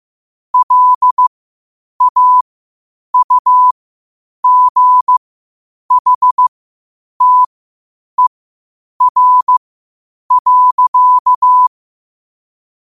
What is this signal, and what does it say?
Morse code, heard as a keyed tone, about 15 wpm, spaced as for 8 wpm, 1000 Hz. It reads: LAUGHTER.
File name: qotdfiles / 8wpm